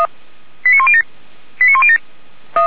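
Five-tone selective call (ZVEI), heard as quick runs of five tones, tones of about 70 ms. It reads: 89107 89107
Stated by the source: Five-tone ZVEI selective call received on an SDR, from the feuerwehr-orlamuende fire brigade audio server.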